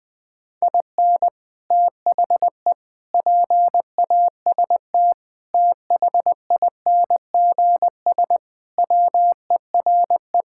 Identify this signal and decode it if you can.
Morse code, heard as a keyed tone, about 20 wpm, 700 Hz. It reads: IN THE PAST THINGS WERE
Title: Sentences with 3-Word Phrases - Spoken Beginning - 20wpm